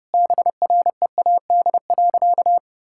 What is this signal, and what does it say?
Morse code, heard as a keyed tone, about 30 wpm, 700 Hz. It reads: BREAD.
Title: Sentences from Top 1600 Words 30wpm